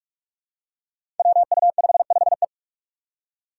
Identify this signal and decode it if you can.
Morse code, heard as a keyed tone, about 44 wpm, 700 Hz. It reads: WU55E
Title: Five Challenging Letter-Number Combinations 45wpm